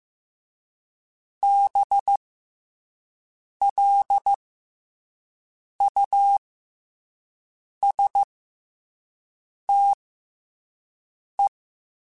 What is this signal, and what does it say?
Morse code, heard as a keyed tone, about 15 wpm, spaced as for 5 wpm, 800 Hz. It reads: BLUSTE
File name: Special INGs-mp3s / Special INGs-mp3s / SlowToFaster ing @10